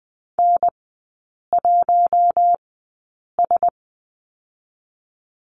Morse code, heard as a keyed tone, about 20 wpm, spaced as for 8 wpm, 700 Hz. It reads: N1S
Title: 081 Lesson 14e - Number 2 - Three Characters 20wpm